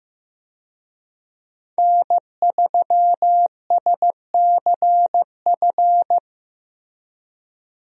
Morse code, heard as a keyed tone, about 15 wpm, 700 Hz. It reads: N3SCF
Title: USA Call Signs - 5 Characters - Repeated 2x - 15wpm